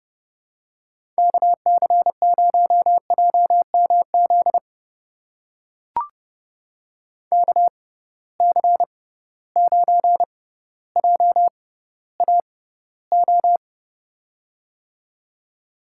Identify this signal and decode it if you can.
Morse code, heard as a keyed tone, about 31 wpm, spaced as for 15 wpm, 700 Hz. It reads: KC0JMZ KC9JAO
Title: USA Call Signs - 6 Characters - 6x Character Spacing - 30wpm